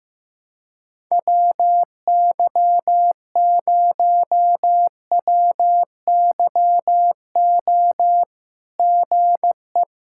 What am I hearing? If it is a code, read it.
Morse code, heard as a keyed tone, about 15 wpm, 700 Hz. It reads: WY0WYO GE
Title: SOTA QSO Practice 15wpm